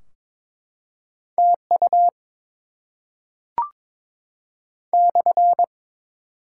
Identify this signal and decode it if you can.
Morse code, heard as a keyed tone, about 22 wpm, 700 Hz. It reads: TU /
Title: Top 90 QSO Elements 22wpm